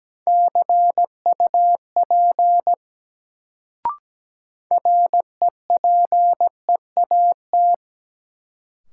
Morse code, heard as a keyed tone, about 17 wpm, 700 Hz. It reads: CUP REPEAT
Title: Sentences from Top 1600 Words - New Word Drill Alt - 17wpm